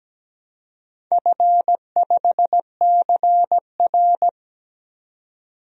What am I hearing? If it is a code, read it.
Morse code, heard as a keyed tone, about 17 wpm, 700 Hz. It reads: F5CR
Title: DXCC-100 Call Signs - 4 Characters - 17wpm